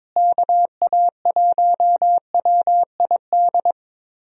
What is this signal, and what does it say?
Morse code, heard as a keyed tone, about 22 wpm, 700 Hz. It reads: KA1WID
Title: USA Call Signs - 6 Characters - Repeated 3x - 22wpm